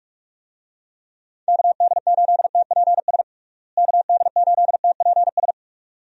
Morse code, heard as a keyed tone, about 46 wpm, 700 Hz. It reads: KD8TPS KD8TPS
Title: USA Call Signs - 6 Characters - Repeated 2x - 45wpm